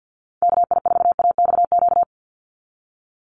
Morse code, heard as a keyed tone, about 48 wpm, 700 Hz. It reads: KI4AXQ
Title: USA Call Signs 50wpm